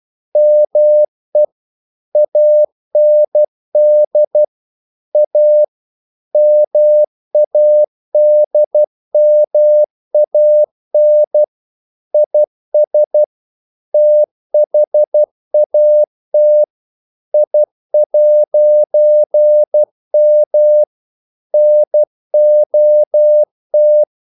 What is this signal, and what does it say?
Morse code, heard as a keyed tone, about 12 wpm, 600 Hz. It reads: ME AND A MADMAN IS THAT I'M NOT
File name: CW / Practice / EnglishQuotations0112